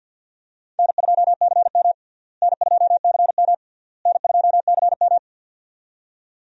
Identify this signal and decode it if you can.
Morse code, heard as a keyed tone, about 52 wpm, 700 Hz. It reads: N2CM N2CM N2CM
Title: USA Call Signs - 4 Characters - Repeated 3x - 50wpm